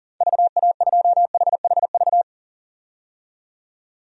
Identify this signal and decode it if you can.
Morse code, heard as a keyed tone, about 37 wpm, 700 Hz. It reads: VA2HHV
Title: DXCC-100 Call Signs - 6 Characters - 40wpm